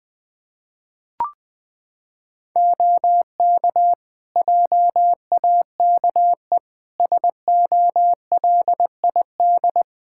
Morse code, heard as a keyed tone, about 20 wpm, 700 Hz. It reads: OK JAKE SOLID
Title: Practice QSO - Exchange 3 - More Info - 20wpm